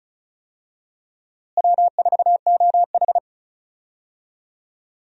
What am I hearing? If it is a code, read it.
Morse code, heard as a keyed tone, about 36 wpm, 700 Hz. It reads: W4OH
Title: USA Call Signs - 4 Characters - Repeated 3x - 35wpm